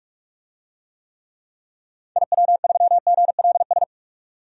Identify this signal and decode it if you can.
Morse code, heard as a keyed tone, about 43 wpm, 700 Hz. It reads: IW3GLS
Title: DXCC-100 Call Signs - 6 Characters - 45wpm